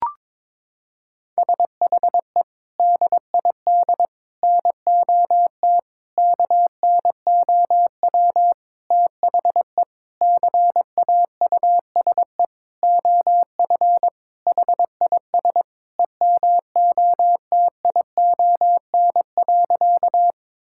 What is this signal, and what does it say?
Morse code, heard as a keyed tone, about 22 wpm, 700 Hz. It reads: SHE DID NOT KNOW THE CAUSE OF HIS EMOTION.
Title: Sentences from Top 1600 Words 22wpm